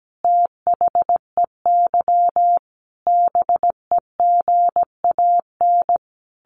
Morse code, heard as a keyed tone, about 17 wpm, 700 Hz. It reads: THEY BEGAN
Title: Sentences from Top 1400 Words 17wpm